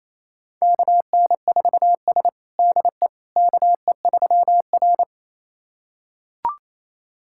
Morse code, heard as a keyed tone, about 27 wpm, 700 Hz. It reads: KN4S DE KE3R
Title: USA Callsign DE Callsign - 4 Characters - 28wpm